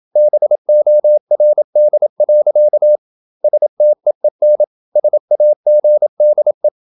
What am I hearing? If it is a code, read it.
Morse code, heard as a keyed tone, about 27 wpm, 600 Hz. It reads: BORD. STEEN SAGDE
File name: Ferie QRQ 27wpm